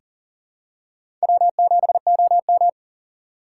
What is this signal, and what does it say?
Morse code, heard as a keyed tone, about 40 wpm, 700 Hz. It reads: W7OM
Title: USA Call Signs - 4 Characters - 2x Character Spacing - 40wpm